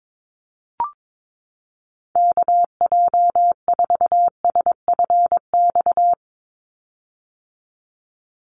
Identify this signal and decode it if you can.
Morse code, heard as a keyed tone, about 22 wpm, 700 Hz. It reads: KJ4SFX
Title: USA Call Signs 22wpm